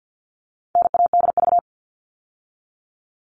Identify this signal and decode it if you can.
Morse code, heard as a keyed tone, about 50 wpm, 700 Hz. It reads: NADV
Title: All Cut Numbers - 4-Digit 50wpm